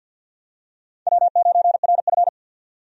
Morse code, heard as a keyed tone, about 52 wpm, 700 Hz. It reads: W9RF
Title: USA Call Signs - 4 Characters - 50wpm